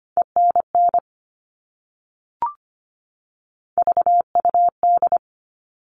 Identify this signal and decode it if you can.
Morse code, heard as a keyed tone, about 25 wpm, 700 Hz. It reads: ENN VUD
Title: All Cut Numbers - 3-Digit 25wpm